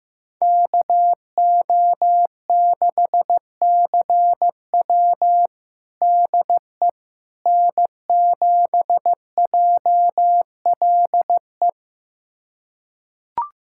Morse code, heard as a keyed tone, about 15 wpm, 700 Hz. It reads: KO6CW DE N7JLE
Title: USA Callsign DE Callsign - 5 Characters - 15wpm